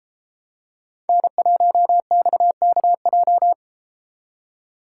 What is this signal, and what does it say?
Morse code, heard as a keyed tone, about 33 wpm, 700 Hz. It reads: N1XKJ